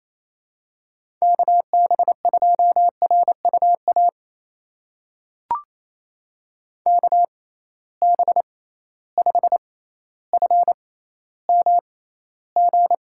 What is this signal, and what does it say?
Morse code, heard as a keyed tone, about 28 wpm, spaced as for 14 wpm, 700 Hz. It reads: KB2RUA KB5FMG
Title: USA Call Signs - 6 Characters - 6x Character Spacing - 28wpm